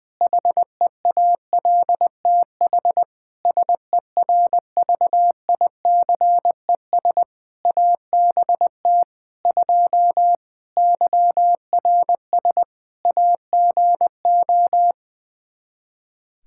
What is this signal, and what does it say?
Morse code, heard as a keyed tone, about 20 wpm, 700 Hz. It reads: HEALTH SERVICES ABT 2 YRS AGO
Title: Practice QSO - Exchange 3 - More Info - 20wpm